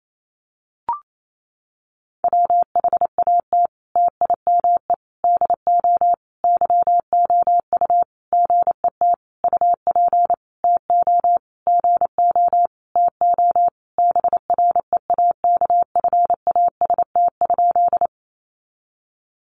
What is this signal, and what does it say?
Morse code, heard as a keyed tone, about 28 wpm, 700 Hz. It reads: WHAT TIME DO YOU GET UP TO GO TO BREAKFAST?
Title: Sentences from Top 1500 Words 28wpm